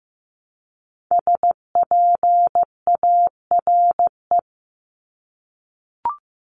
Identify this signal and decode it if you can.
Morse code, heard as a keyed tone, about 15 wpm, 700 Hz. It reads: SPARE
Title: Top 1751-2000 Words 15wpm